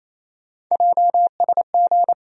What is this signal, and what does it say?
Morse code, heard as a keyed tone, about 28 wpm, 700 Hz. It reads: JSG
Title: USA Call Sign Postfixes - 3 Letters 28wpm